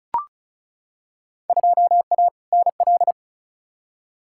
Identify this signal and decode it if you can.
Morse code, heard as a keyed tone, about 36 wpm, 700 Hz. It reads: 2A NL
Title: Field Day Exchanges Encore - 35wpm